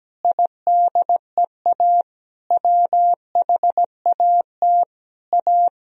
Morse code, heard as a keyed tone, about 17 wpm, 700 Hz. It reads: IDEA WHAT A
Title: Sentences from Top 1000 Words - Review Part 1 - 17wpm